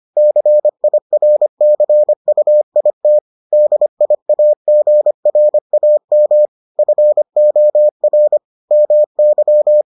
Morse code, heard as a keyed tone, about 25 wpm, 600 Hz. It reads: CIRCUIT DIAGRAM FOR MY